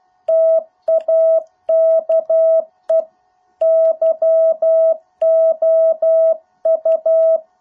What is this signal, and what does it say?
Morse code, heard as a keyed tone, about 12 wpm, 650 Hz. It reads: TAKE YOU